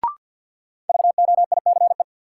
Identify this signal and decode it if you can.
Morse code, heard as a keyed tone, about 50 wpm, 700 Hz. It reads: VOICE
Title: Sentences from Top 500 Words - New Word Drill - 50wpm